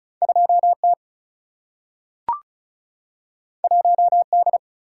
Morse code, heard as a keyed tone, about 36 wpm, 700 Hz. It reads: 2 T 1 D
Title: Single-Letter Single-Number 35wpm